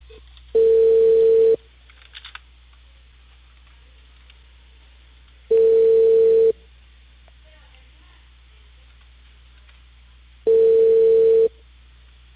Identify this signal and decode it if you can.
Ringback tone (China, 450 Hz), with speech.